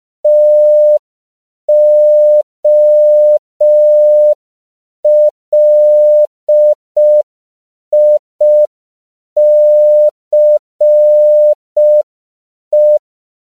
Morse code, heard as a keyed tone, about 5 wpm, 600 Hz. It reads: TOLICE
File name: CW / 5PPM